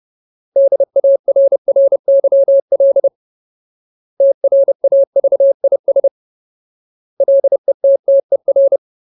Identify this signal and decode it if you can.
Morse code, heard as a keyed tone, about 30 wpm, 550 Hz. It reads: DARRYL TRAVIS LETTER
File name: Ultimate Word List / 30WPM Ultimate